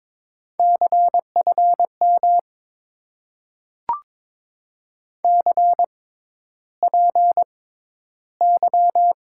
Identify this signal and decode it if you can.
Morse code, heard as a keyed tone, about 22 wpm, spaced as for 11 wpm, 700 Hz. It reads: CFM CPY
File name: Top 90 QSO Elements - 6x Character Spacing - 22wpm